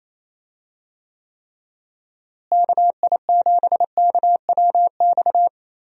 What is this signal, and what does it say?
Morse code, heard as a keyed tone, about 28 wpm, 700 Hz. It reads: KI7KWX
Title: USA Call Signs - 6 Characters - Repeated 3x - 28wpm